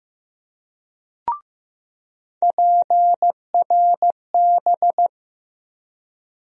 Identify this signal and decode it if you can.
Morse code, heard as a keyed tone, about 15 wpm, 700 Hz. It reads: PRB
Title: USA Call Sign Postfixes - 3 Letters 15wpm